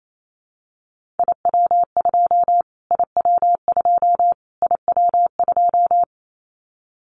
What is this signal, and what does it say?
Morse code, heard as a keyed tone, about 28 wpm, 700 Hz. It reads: IW2 IW2 IW2
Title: Callsign Prefixes for Top 25 DXCC - Repeated 3x - 28wpm